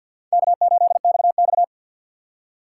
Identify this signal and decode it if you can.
Morse code, heard as a keyed tone, about 53 wpm, 700 Hz. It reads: K8XX
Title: USA Call Signs - 2x Character Spacing - 50wpm